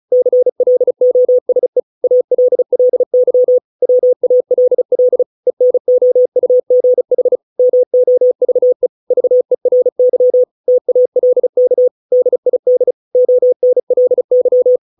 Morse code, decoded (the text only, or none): CLOSE ALLY WALL ENOUGH MOVE VERY TALK DID ONLY